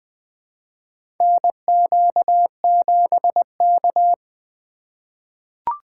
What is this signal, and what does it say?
Morse code, heard as a keyed tone, about 20 wpm, 700 Hz. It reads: NQ7K
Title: USA Call Signs - 4 Characters - 2x Character Spacing - 20wpm